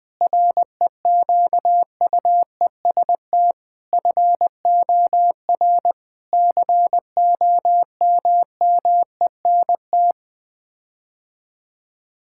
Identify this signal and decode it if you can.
Morse code, heard as a keyed tone, about 21 wpm, 700 Hz. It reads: REQUEST FOR COMMENT
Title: Sets of 8 Words - 20wpm